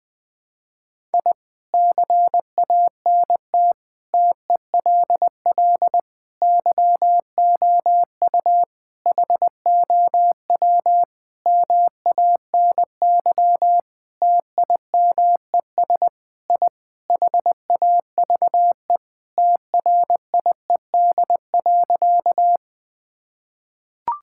Morse code, heard as a keyed tone, about 20 wpm, 700 Hz. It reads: I CANT TELL YOU HOW MANY TIMES I HAVE TRIED.
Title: Sentences with 7-Word Phrases - Spoken Beginning - 20wpm